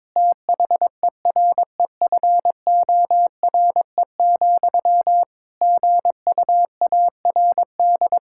Morse code, heard as a keyed tone, about 22 wpm, 700 Hz. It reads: THEREFORE, GUARD
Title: Sentences from Top 1600 Words 22wpm